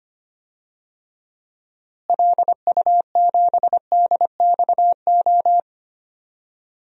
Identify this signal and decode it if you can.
Morse code, heard as a keyed tone, about 25 wpm, 700 Hz. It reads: LU7DXO
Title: DXCC-100 Call Signs - 6 Characters - Repeated 2x - 25wpm